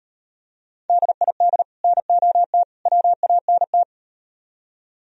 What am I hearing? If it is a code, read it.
Morse code, heard as a keyed tone, about 38 wpm, 700 Hz. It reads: DID NOT WANT